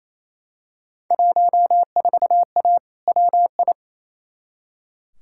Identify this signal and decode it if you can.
Morse code, heard as a keyed tone, about 28 wpm, 700 Hz. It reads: 14A WI
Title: Field Day Exchanges Encore - 28wpm